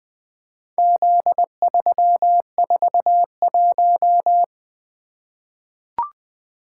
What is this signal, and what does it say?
Morse code, heard as a keyed tone, about 20 wpm, 700 Hz. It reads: Z341
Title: Four Letter-Number Combinations - 20wpm